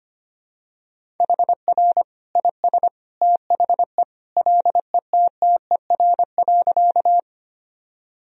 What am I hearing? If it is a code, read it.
Morse code, heard as a keyed tone, about 25 wpm, 700 Hz. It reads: HR IS THE LETTER.
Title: Sentences from Top 300 Words and Abbreviations 25wpm